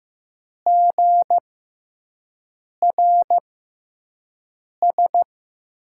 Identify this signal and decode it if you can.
Morse code, heard as a keyed tone, about 15 wpm, spaced as for 5 wpm, 700 Hz. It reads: GRS
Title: USA Call Signs - 6x Character Spacing - 15wpm